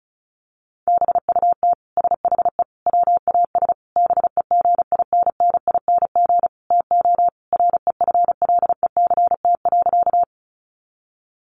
Morse code, heard as a keyed tone, about 34 wpm, 700 Hz. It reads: BUT SHE WAS BEGINNING TO REFLECT.
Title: Sentences from Top 1500 Words 35wpm